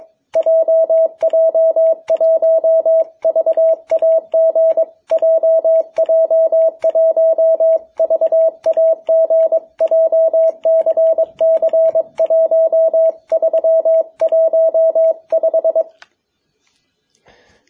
Morse code, heard as a keyed tone, about 22 wpm, 660 Hz. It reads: JJ1VAG JJ1VAG JCC1315
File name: jh8cbh / 2021cw